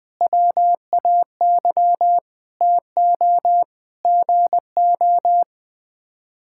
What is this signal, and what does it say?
Morse code, heard as a keyed tone, about 20 wpm, 700 Hz. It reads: WAY TO GO
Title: Sets of 9 Words - 20wpm